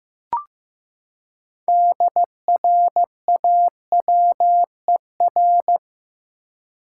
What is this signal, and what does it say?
Morse code, heard as a keyed tone, about 15 wpm, 700 Hz. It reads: DRAWER
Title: Six Letter Words - 15wpm